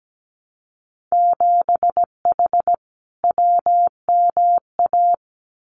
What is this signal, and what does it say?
Morse code, heard as a keyed tone, about 17 wpm, 700 Hz. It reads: 7H WMA
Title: Winter Field Day Exchanges Encore 17wpm